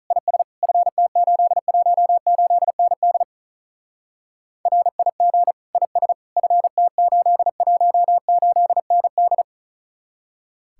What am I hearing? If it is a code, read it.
Morse code, heard as a keyed tone, about 45 wpm, 700 Hz. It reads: IS FT818ND RIG IS FT818ND